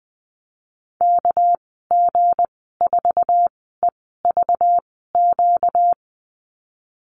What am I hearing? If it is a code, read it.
Morse code, heard as a keyed tone, about 20 wpm, spaced as for 15 wpm, 700 Hz. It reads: KG4EVQ
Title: 238 Lesson 38g - Letter q - Call Signs 20wpm